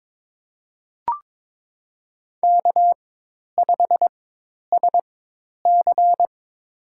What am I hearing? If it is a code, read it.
Morse code, heard as a keyed tone, about 22 wpm, spaced as for 10 wpm, 700 Hz. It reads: K5SC
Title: USA Call Signs - 4 Characters - 4x Character Spacing - 22wpm